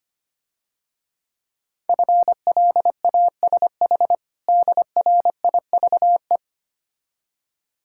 Morse code, heard as a keyed tone, about 25 wpm, 700 Hz. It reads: FLASH DRIVE
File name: Easy Words - Speed-Racing with Hints - 25wpm